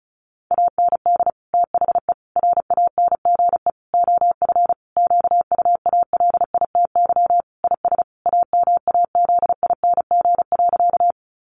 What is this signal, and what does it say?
Morse code, heard as a keyed tone, about 34 wpm, 700 Hz. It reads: AND THE RANGE OF QUALITY IS AMAZING.
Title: Sentences from Top 1500 Words 35wpm